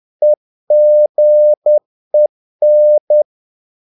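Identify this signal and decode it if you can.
Morse code, heard as a keyed tone, about 10 wpm, 600 Hz. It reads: EGEN
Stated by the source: Morse practice of short Danish words at 10 wpm, from CW med Gnister.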